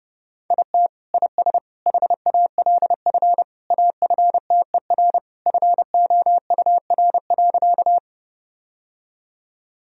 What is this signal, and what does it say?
Morse code, heard as a keyed tone, about 32 wpm, 700 Hz. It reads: IT IS HALF AFTER FOUR.
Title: Sentences from Top 1000 Words - Review Part 1 - 30wpm